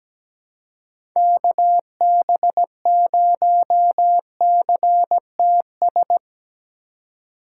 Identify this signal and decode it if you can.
Morse code, heard as a keyed tone, about 17 wpm, 700 Hz. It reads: KB0CTS
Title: USA Call Signs - 6 Characters - 4x Character Spacing - 17wpm